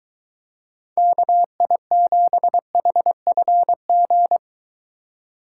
Morse code, heard as a keyed tone, about 22 wpm, 700 Hz. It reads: KI7HFG